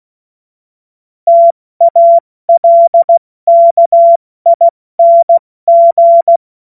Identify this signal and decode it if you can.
Morse code, heard as a keyed tone, about 16 wpm, 680 Hz. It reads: TALKING
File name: Morsekurs / Witze / Witz 23 Januar 2024 70 F80